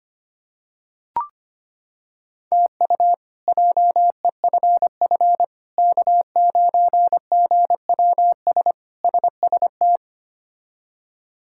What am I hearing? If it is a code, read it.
Morse code, heard as a keyed tone, about 25 wpm, 700 Hz. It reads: TU JEFF K9GWS SST
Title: K1USN SST QSO Practice - 25wpm